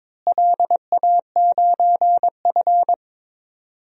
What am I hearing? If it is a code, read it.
Morse code, heard as a keyed tone, about 22 wpm, 700 Hz. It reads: LA9F
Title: DXCC-100 Call Signs - 4 Characters - Repeated 2x - 22wpm